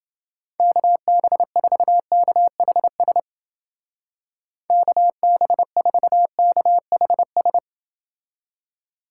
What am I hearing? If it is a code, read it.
Morse code, heard as a keyed tone, about 29 wpm, 700 Hz. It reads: KB4KHS KB4KHS